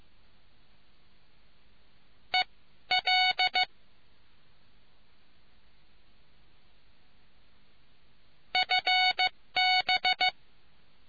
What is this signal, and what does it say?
Morse code, heard as a keyed tone, about 15 wpm, spaced as for 12 wpm, 740 Hz. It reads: EL FB